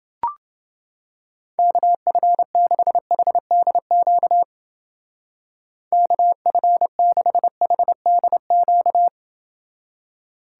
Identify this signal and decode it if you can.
Morse code, heard as a keyed tone, about 29 wpm, 700 Hz. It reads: KF6HDQ KF6HDQ